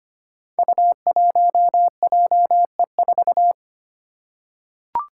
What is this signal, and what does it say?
Morse code, heard as a keyed tone, about 25 wpm, 700 Hz. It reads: U1JE4
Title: Five Challenging Letter-Number Combinations 25wpm